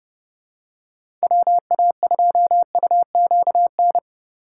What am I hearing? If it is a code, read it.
Morse code, heard as a keyed tone, about 31 wpm, 700 Hz. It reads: WA2UQN
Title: USA Call Signs - 6 Characters - 3x Character Spacing - 30wpm